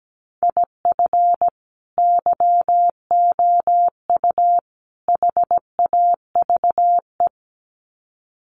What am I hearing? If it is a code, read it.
Morse code, heard as a keyed tone, about 17 wpm, 700 Hz. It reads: IF YOU HAVE
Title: Sets of 3 Words 17wpm